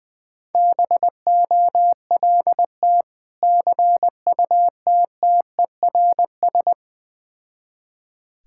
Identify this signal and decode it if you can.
Morse code, heard as a keyed tone, about 20 wpm, 700 Hz. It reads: BOLT CUTTERS
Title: Difficult Words with Hints - 20wpm